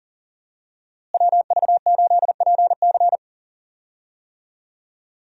Morse code, heard as a keyed tone, about 40 wpm, 700 Hz. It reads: WV8PC
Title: USA Call Signs - 5 Characters - Repeated 3x - 40wpm